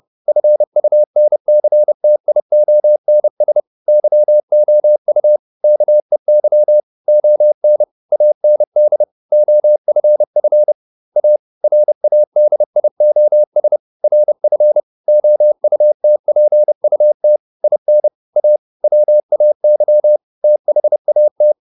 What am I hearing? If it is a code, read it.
Morse code, heard as a keyed tone, about 30 wpm, 600 Hz. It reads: FUNCTIONS YOU KEY ON AND OFF A RADIOS RF OUTPUT IN A WAY THAT